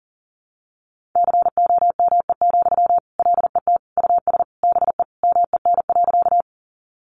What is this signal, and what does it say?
Morse code, heard as a keyed tone, about 40 wpm, 700 Hz. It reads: COME, LET US BE MEN.
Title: Sentences from Top 300 Words 40wpm